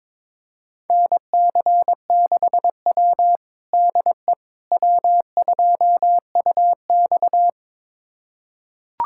Morse code, heard as a keyed tone, about 22 wpm, 700 Hz. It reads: NC6W DE W2UX
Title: USA Callsign DE Callsign - 4 Characters - 22wpm